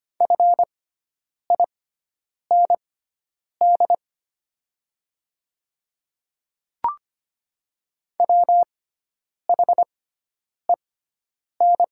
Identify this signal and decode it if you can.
Morse code, heard as a keyed tone, about 25 wpm, spaced as for 9 wpm, 700 Hz. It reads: FIND WHEN
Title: 6x Character Spacing - Top 300 Words - 25wpm